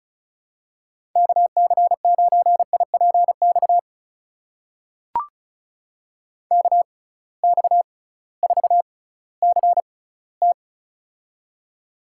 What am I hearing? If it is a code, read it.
Morse code, heard as a keyed tone, about 34 wpm, spaced as for 19 wpm, 700 Hz. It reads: KC9IPX KX4CT
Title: USA Call Signs - 6x Character Spacing - 35wpm